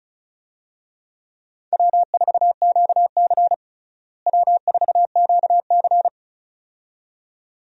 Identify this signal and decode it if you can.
Morse code, heard as a keyed tone, about 36 wpm, 700 Hz. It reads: W4QC W4QC
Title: USA Call Signs - 4 Characters - Repeated 2x - 35wpm